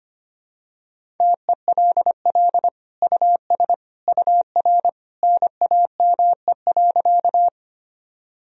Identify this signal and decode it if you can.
Morse code, heard as a keyed tone, about 25 wpm, 700 Hz. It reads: TELL US UR NAME.
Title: Sentences from Top 200 Words and Abbreviations 25wpm